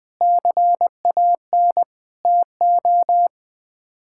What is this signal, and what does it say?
Morse code, heard as a keyed tone, about 20 wpm, 700 Hz. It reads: CAN TO
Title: Sets of 8 Words - 20wpm